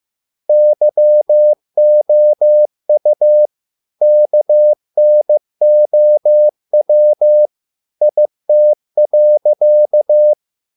Morse code, heard as a keyed tone, about 15 wpm, 600 Hz. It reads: YOU KNOW IT.